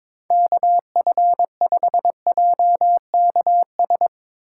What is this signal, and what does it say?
Morse code, heard as a keyed tone, about 22 wpm, 700 Hz. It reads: KF5JKS
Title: USA Call Signs - 4x Character Spacing - 22wpm